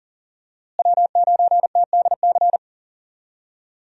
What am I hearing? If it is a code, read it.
Morse code, heard as a keyed tone, about 40 wpm, 700 Hz. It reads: W9TDC
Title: USA Call Signs - 5 Characters - Repeated 2x - 40wpm